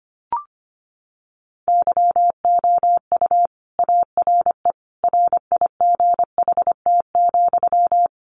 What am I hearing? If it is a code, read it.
Morse code, heard as a keyed tone, about 25 wpm, 700 Hz. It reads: YOU ARE RIGHT,
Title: Sentences from Top 300 Words 25wpm